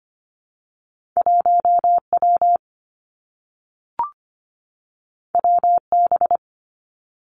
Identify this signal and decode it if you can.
Morse code, heard as a keyed tone, about 25 wpm, 700 Hz. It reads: 1 W W B
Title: Two Challenging Letter-Number Combinations 25wpm